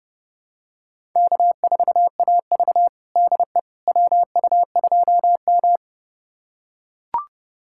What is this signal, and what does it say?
Morse code, heard as a keyed tone, about 30 wpm, 700 Hz. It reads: K4AV DE WU2M
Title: USA Callsign DE Callsign - 4 Characters - 30wpm